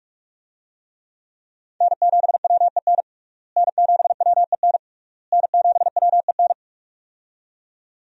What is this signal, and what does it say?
Morse code, heard as a keyed tone, about 42 wpm, 700 Hz. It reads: N7WEN N7WEN N7WEN
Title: USA Call Signs - 5 Characters - Repeated 3x - 45wpm